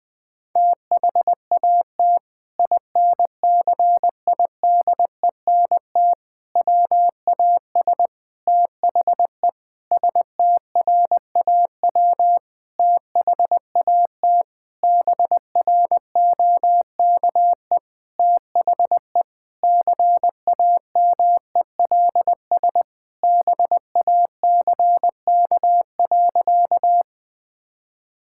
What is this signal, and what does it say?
Morse code, heard as a keyed tone, about 20 wpm, 700 Hz. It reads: THAT INCIDENT WAS THE STRAW THAT BROKE THE CAMELS BACK.
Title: Sentences with 8-Word Phrases - 20wpm